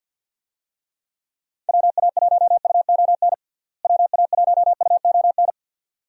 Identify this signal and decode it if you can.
Morse code, heard as a keyed tone, about 50 wpm, 700 Hz. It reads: WA1UON WA1UON
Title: USA Call Signs - 6 Characters - Repeated 2x - 50wpm